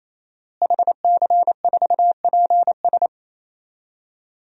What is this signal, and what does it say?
Morse code, heard as a keyed tone, about 28 wpm, 700 Hz. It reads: HC4PS
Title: DXCC-100 Call Signs - 5 Characters - Repeated 3x - 28wpm